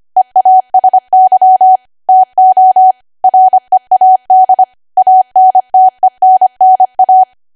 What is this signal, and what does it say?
Morse code, heard as a keyed tone, about 25 wpm, 750 Hz. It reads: EASY TO READ ANTENNA
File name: Archive / 25 WPM / 25WPM